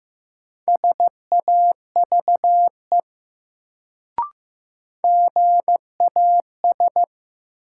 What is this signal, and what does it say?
Morse code, heard as a keyed tone, about 15 wpm, 700 Hz. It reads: SAVE GAS
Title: Sentences from Top 1100 Words - New Word Drill - 15wpm